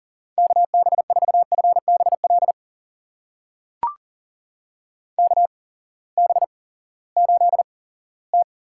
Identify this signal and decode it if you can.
Morse code, heard as a keyed tone, about 43 wpm, spaced as for 20 wpm, 700 Hz. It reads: KB4FBL KB8T
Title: USA Call Signs - 6 Characters - 8x Character Spacing - 40wpm